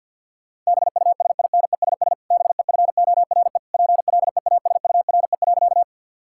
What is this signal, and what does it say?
Morse code, heard as a keyed tone, about 50 wpm, 700 Hz. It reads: BUSINESS BEFORE PLEASURE.
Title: Sentences from Top 1600 Words 50wpm